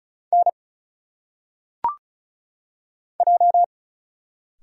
Morse code, heard as a keyed tone, about 36 wpm, 700 Hz. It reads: N J
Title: Single Letters - Phonetic Pronunciation 35wpm